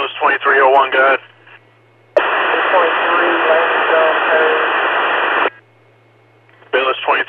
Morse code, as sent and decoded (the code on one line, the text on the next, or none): none